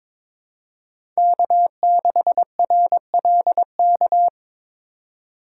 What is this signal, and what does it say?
Morse code, heard as a keyed tone, about 22 wpm, 700 Hz. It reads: K6RLK
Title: USA Call Signs - 5 Characters - 2x Character Spacing - 22wpm